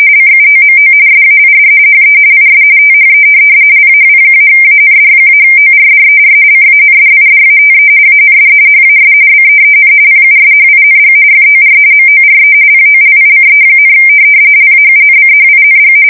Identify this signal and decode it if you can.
RTTY, about 45.45 baud. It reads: CUOMO SAID HE IS THE LAW-AND-ORDER CANDIDATE AND HAS BEEN PUSHING FOR MORE POLICE, LAUNCHING
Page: 170 Shift 2125/2295